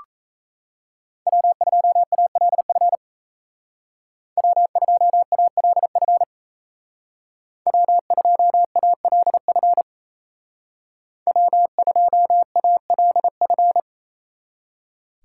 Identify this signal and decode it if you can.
Morse code, heard as a keyed tone, about 36 wpm, 700 Hz. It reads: W2ALF W2ALF W2ALF W2ALF